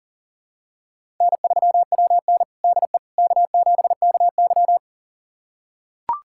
Morse code, heard as a keyed tone, about 37 wpm, 700 Hz. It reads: N3WN DE K7KY
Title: USA Callsign DE Callsign - 4 Characters - 40wpm